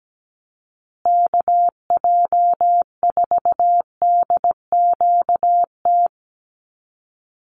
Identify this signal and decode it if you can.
Morse code, heard as a keyed tone, about 17 wpm, 700 Hz. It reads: KJ4DQT
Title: USA Call Signs Repeated 3x 17wpm